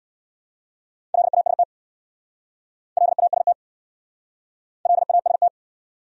Morse code, heard as a keyed tone, about 63 wpm, 700 Hz. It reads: LAST LAST LAST